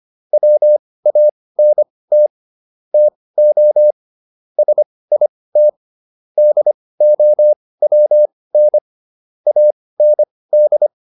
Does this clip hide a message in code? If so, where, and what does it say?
Yes: Morse code from 0.3 to 10.9 s, heard as a keyed tone, about 25 wpm, spaced as for 18 wpm, 600 Hz. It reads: WANT TO SIT DOWN AND